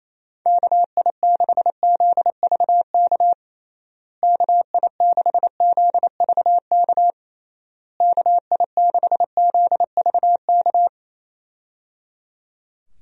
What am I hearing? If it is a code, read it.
Morse code, heard as a keyed tone, about 28 wpm, 700 Hz. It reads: KI6ZVK KI6ZVK KI6ZVK
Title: USA Call Signs - 6 Characters - Repeated 3x - 28wpm